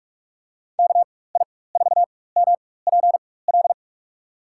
Morse code, heard as a keyed tone, about 41 wpm, spaced as for 20 wpm, 700 Hz. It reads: KI4MPL
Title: USA Call Signs - 6 Characters - 4x Character Spacing - 45wpm